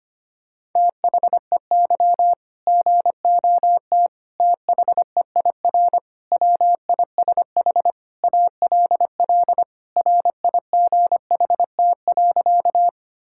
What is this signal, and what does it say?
Morse code, heard as a keyed tone, about 25 wpm, 700 Hz. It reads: THEY GOT THEIR WISH ALL RIGHT.
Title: Sentences from Top 1600 Words 25wpm